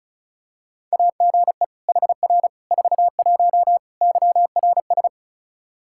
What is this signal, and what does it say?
Morse code, heard as a keyed tone, about 36 wpm, 700 Hz. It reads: AGE HR 41 YRS
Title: Practice QSO - Exchange 3 - More Info - 35wpm